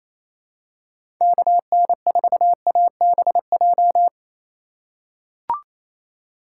Morse code, heard as a keyed tone, about 28 wpm, 700 Hz. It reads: KN4ABJ
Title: USA Call Signs - 6 Characters - 3x Character Spacing - 28wpm